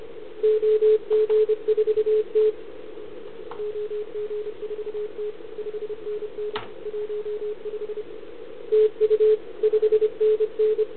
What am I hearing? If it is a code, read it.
Morse code, heard as a keyed tone, about 28 wpm, 420 Hz. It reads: OG4T OG4T 5NN 15 TU 5NN
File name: CQ160 2022 CW / RUN